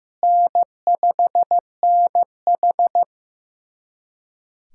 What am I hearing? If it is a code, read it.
Morse code, heard as a keyed tone, about 15 wpm, 700 Hz. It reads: N5NH
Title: USA Callsign DE Callsign - 4 Characters - 15wpm